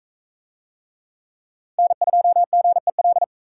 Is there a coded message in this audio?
Yes: Morse code from 1.8 to 3.2 s, heard as a keyed tone, about 42 wpm, 700 Hz. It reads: N2GEL